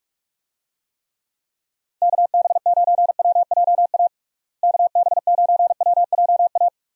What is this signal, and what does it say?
Morse code, heard as a keyed tone, about 46 wpm, 700 Hz. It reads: KB9WJA KB9WJA
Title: USA Call Signs - 6 Characters - Repeated 3x - 45wpm